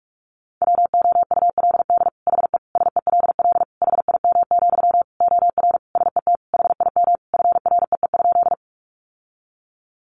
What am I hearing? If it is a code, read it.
Morse code, heard as a keyed tone, about 44 wpm, 700 Hz. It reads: WOULD HE SELL HIM, OR SET HIM FREE?
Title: Sentences from Top 1100 Words 45wpm